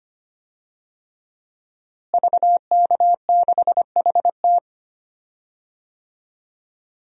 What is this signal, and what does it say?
Morse code, heard as a keyed tone, about 25 wpm, 700 Hz. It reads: VK6HT
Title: DXCC-100 Call Signs - 5 Characters - Repeated 3x - 25wpm